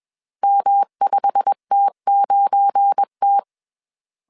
Morse code, heard as a keyed tone, about 21 wpm, 800 Hz. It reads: M5T9T
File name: CW / DET / 1 9 1500C S C 20wpm 2310